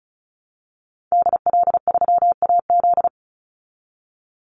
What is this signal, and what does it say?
Morse code, heard as a keyed tone, about 34 wpm, 700 Hz. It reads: DL3AZ
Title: Call Signs Repeated 3x - 35wpm